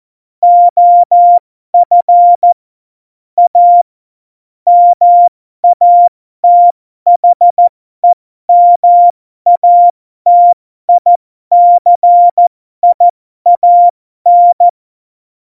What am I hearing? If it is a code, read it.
Morse code, heard as a keyed tone, about 14 wpm, spaced as for 12 wpm, 700 Hz. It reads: OF A MATHEMATICIAN
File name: Morsekurs / Witze